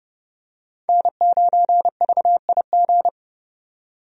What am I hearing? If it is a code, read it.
Morse code, heard as a keyed tone, about 30 wpm, 700 Hz. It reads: N9VIG
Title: USA Call Signs - 5 Characters - 30wpm